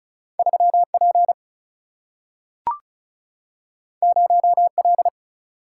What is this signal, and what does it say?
Morse code, heard as a keyed tone, about 36 wpm, 700 Hz. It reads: 3 P 0 L
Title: Single-Letter Single-Number 35wpm